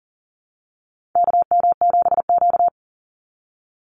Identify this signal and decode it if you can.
Morse code, heard as a keyed tone, about 40 wpm, 700 Hz. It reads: KM7Q
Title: Call Signs - 40wpm